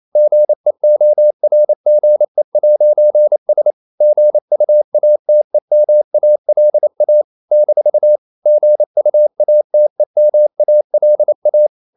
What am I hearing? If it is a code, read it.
Morse code, heard as a keyed tone, about 28 wpm, 600 Hz. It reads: GEORGE'S GUATEMALA - GUATEMALA
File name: CW / Practice / NationalCapitals28